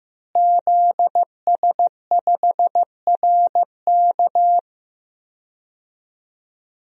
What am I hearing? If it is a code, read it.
Morse code, heard as a keyed tone, about 15 wpm, 700 Hz. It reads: ZS5RK
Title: DXCC-100 Call Signs - 5 Characters - Repeated 2x - 15wpm